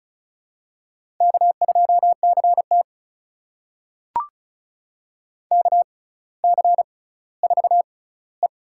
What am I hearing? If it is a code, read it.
Morse code, heard as a keyed tone, about 33 wpm, spaced as for 18 wpm, 700 Hz. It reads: K2CT KC4E
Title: USA Call Signs - 6x Character Spacing - 35wpm